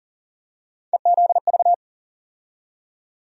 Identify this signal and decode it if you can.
Morse code, heard as a keyed tone, about 40 wpm, 700 Hz. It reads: EZV
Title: USA Call Sign Postfixes - 3 Letters 40wpm